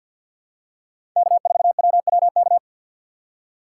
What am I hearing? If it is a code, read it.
Morse code, heard as a keyed tone, about 47 wpm, 700 Hz. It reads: K4WWK
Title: USA Call Signs - 5 Characters - 3x Character Spacing - 50wpm